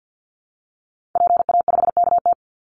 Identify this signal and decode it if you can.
Morse code, heard as a keyed tone, about 49 wpm, 700 Hz. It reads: PA5KT
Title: Call Signs Repeated 3x - 50wpm